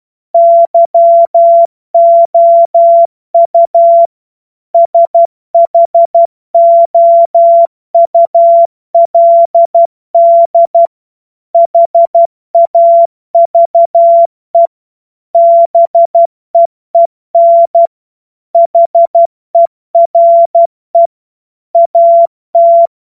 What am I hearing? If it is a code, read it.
Morse code, heard as a keyed tone, about 12 wpm, 680 Hz. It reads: YOU SHOULD HAVE BEEN HERE AT
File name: Morsekurs / Witze